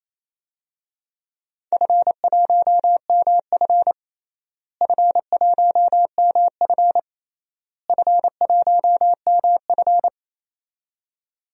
Morse code, heard as a keyed tone, about 27 wpm, 700 Hz. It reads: F1MF F1MF F1MF
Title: DXCC-100 Call Signs - 4 Characters - Repeated 3x - 28wpm